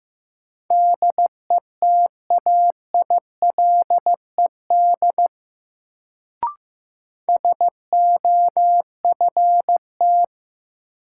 Morse code, heard as a keyed tone, about 15 wpm, 700 Hz. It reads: DETAILED SOFT
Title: Sentences from Top 1500 Words - New Word Drill Alt - 15wpm